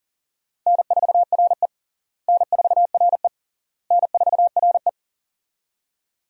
Morse code, heard as a keyed tone, about 37 wpm, 700 Hz. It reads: N4RE N4RE N4RE
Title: USA Call Signs - 4 Characters - Repeated 3x - 40wpm